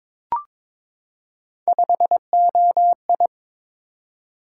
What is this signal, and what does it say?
Morse code, heard as a keyed tone, about 22 wpm, 700 Hz. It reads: 5OI
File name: practice / 22wpm / Three Letter-Number Combinations - 22wpm